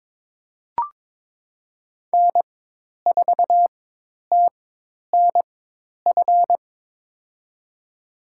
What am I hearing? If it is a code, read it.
Morse code, heard as a keyed tone, about 22 wpm, spaced as for 10 wpm, 700 Hz. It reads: N4TNF
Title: USA Call Signs - 4x Character Spacing - 22wpm